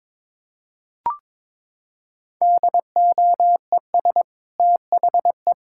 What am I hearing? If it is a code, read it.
Morse code, heard as a keyed tone, about 22 wpm, 700 Hz. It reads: DOES THE
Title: Sentences from Top 1000 Words - Review Part 5 - 22wpm